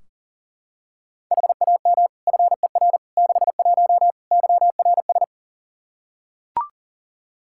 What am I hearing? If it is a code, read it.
Morse code, heard as a keyed tone, about 40 wpm, 700 Hz. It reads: HAM FER 61 YRS
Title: Practice QSO - Exchange 3 - More Info - 40wpm